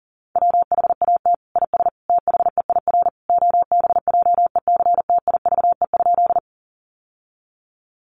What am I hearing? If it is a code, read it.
Morse code, heard as a keyed tone, about 40 wpm, 700 Hz. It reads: WHAT IS THEIR OBJECTIVE?
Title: Sentences from Top 1200 Words 40wpm